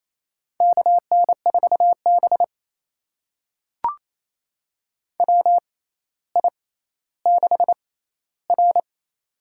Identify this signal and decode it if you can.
Morse code, heard as a keyed tone, about 28 wpm, spaced as for 14 wpm, 700 Hz. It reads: KN4B WI6R
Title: USA Call Signs - 4 Characters - 6x Character Spacing - 28wpm